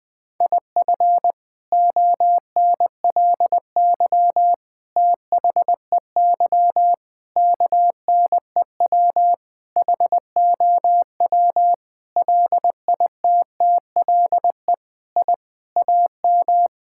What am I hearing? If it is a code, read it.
Morse code, heard as a keyed tone, about 20 wpm, 700 Hz. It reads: IF ONLY THEY KNEW HOW LITTLE I AM
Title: Sentences from Top 1000 Words - Review Part 6 - 20wpm